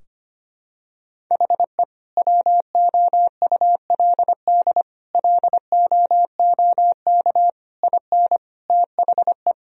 Morse code, heard as a keyed tone, about 25 wpm, 700 Hz. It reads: HE WOULD LOOK IN THE
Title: Sentences from Top 1000 Words - Review Part 3 - 25wpm